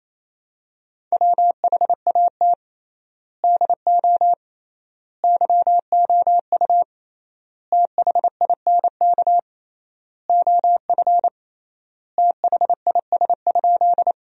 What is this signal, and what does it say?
Morse code, heard as a keyed tone, about 28 wpm, 700 Hz. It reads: WHAT DO YOU THINK OF THIS?
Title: 3x Word-Spacing - Sentences from Top 200 Words - 28wpm